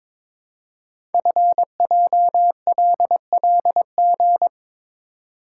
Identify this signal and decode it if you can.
Morse code, heard as a keyed tone, about 22 wpm, 700 Hz. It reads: FJLLG
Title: Code Groups - 5 Letters - 22wpm